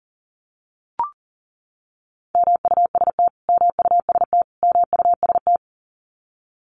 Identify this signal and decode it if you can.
Morse code, heard as a keyed tone, about 40 wpm, 700 Hz. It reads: MUST MUST MUST
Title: Top 500 Words Repeated 40wpm